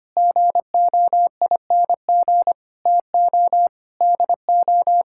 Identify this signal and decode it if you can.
Morse code, heard as a keyed tone, about 25 wpm, 700 Hz. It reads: GOING TO DO
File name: practice / 25wpm / Sentences with 3-Word Phrases - 25wpm